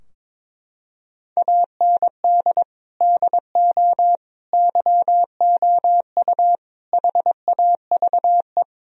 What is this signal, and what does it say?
Morse code, heard as a keyed tone, about 23 wpm, 700 Hz. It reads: AND DO YOU HAVE
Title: Sentences from Top 1000 Words - Review Part 3 - 22wpm